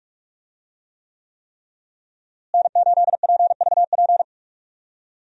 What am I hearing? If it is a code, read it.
Morse code, heard as a keyed tone, about 41 wpm, 700 Hz. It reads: N8PVP
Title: USA Call Signs - 5 Characters - Repeated 3x - 45wpm